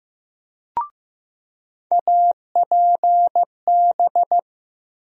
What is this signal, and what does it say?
Morse code, heard as a keyed tone, about 15 wpm, 700 Hz. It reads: APB
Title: USA Call Sign Postfixes - 3 Letters 15wpm